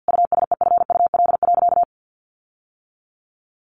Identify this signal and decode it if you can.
Morse code, heard as a keyed tone, about 48 wpm, 700 Hz. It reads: USEFUL.
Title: Sentences from Top 1500 Words 50wpm